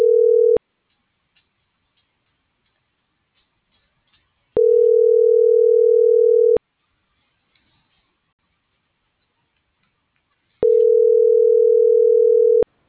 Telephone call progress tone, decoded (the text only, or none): ringback tone